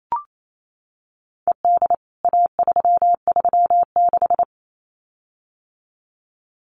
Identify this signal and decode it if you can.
Morse code, heard as a keyed tone, about 28 wpm, 700 Hz. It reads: ED A336
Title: CWT Exchange 28wpm